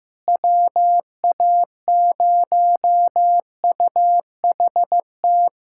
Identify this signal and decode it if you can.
Morse code, heard as a keyed tone, about 16 wpm, 700 Hz. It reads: WA0UHT
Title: USA Callsign DE Callsign - 6 Characters - 15wpm